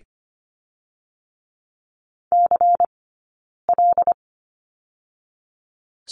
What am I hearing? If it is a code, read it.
Morse code, heard as a keyed tone, about 25 wpm, 700 Hz. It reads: C L
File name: course / 25:10wpm / 092 Lesson 16d - Letter c - Two Characters 25wpm